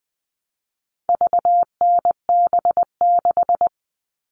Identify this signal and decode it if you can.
Morse code, heard as a keyed tone, about 20 wpm, 700 Hz. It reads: VNB6
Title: All Cut Numbers - 4-Digit 20wpm